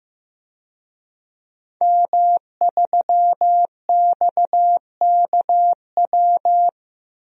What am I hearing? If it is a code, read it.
Morse code, heard as a keyed tone, about 15 wpm, 700 Hz. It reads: M3XKW
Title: DXCC-100 Call Signs - 5 Characters - 15wpm